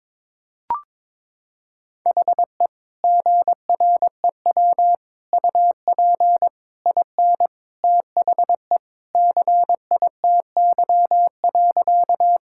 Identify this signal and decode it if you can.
Morse code, heard as a keyed tone, about 22 wpm, 700 Hz. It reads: HE GREW UP IN THE CITY.
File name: practice / 22wpm / Sentences with 3-Word Phrases - 22wpm